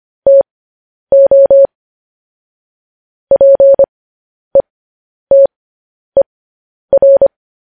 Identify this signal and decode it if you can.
Morse code, heard as a keyed tone, about 25 wpm, spaced as for 10 wpm, 550 Hz. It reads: TO PETER